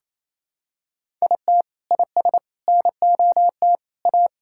Morse code, heard as a keyed tone, about 27 wpm, 700 Hz. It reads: IT IS NOT A
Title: Sentences from Top 1000 Words - Review Part 6 - 28wpm